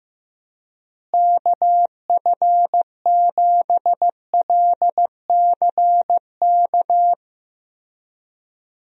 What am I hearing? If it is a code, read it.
Morse code, heard as a keyed tone, about 15 wpm, 700 Hz. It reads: KF7LCK
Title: USA Call Signs - 6 Characters - Repeated 3x - 15wpm